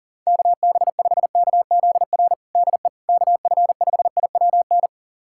Morse code, heard as a keyed tone, about 40 wpm, 700 Hz. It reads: KB5KZR DE KF5IWN
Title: USA Callsign DE Callsign - 6 Characters - 40wpm